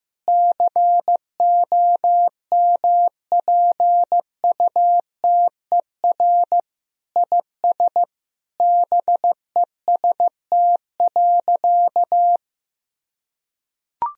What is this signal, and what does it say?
Morse code, heard as a keyed tone, about 15 wpm, 700 Hz. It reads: COMPUTER IS BEST.
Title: Sentences from Top 1000 Words - Review Part 3 - 15wpm